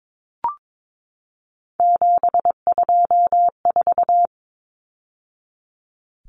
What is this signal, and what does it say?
Morse code, heard as a keyed tone, about 22 wpm, 700 Hz. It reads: 724
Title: Numbers - 3-Digit 22wpm